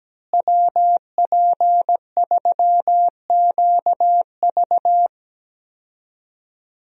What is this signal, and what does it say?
Morse code, heard as a keyed tone, about 17 wpm, 700 Hz. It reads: WP3QV
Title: USA Call Signs - 5 Characters - Repeated 3x - 17wpm